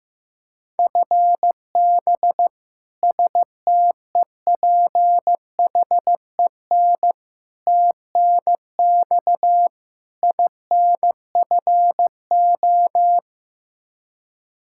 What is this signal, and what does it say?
Morse code, heard as a keyed tone, about 15 wpm, 700 Hz. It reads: FB STEPHEN TNX INFO
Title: Practice QSO - Exchange 3 - More Info - 15wpm